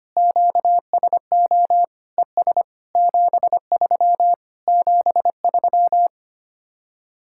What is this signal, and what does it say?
Morse code, heard as a keyed tone, about 24 wpm, 700 Hz. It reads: QSO ES 73 73
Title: Speed-Racing Common QSO Phrases - 25wpm